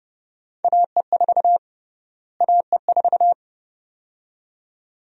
Morse code, heard as a keyed tone, about 29 wpm, 700 Hz. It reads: AE4 AE4
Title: USA Call Sign Prefixes - Repeated 2x - 30wpm